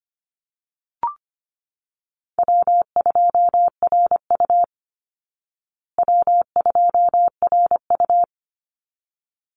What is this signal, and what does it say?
Morse code, heard as a keyed tone, about 25 wpm, 700 Hz. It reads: W2RU W2RU
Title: Call Signs Repeated 3x - 25wpm